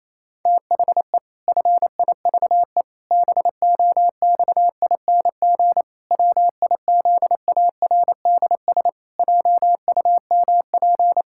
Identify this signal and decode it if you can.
Morse code, heard as a keyed tone, about 28 wpm, 700 Hz. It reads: THE FIVE BOXING WIZARDS JUMP
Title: Pangram Challenge v2 - 28wpm